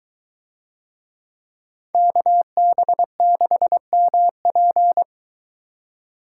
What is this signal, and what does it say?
Morse code, heard as a keyed tone, about 22 wpm, 700 Hz. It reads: KB6MP